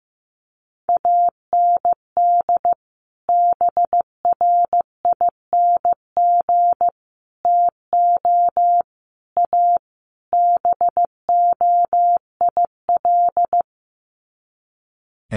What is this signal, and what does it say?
Morse code, heard as a keyed tone, about 15 wpm, 700 Hz. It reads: AND BRING TO A BOIL
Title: Sets of 5 Words Encore 15wpm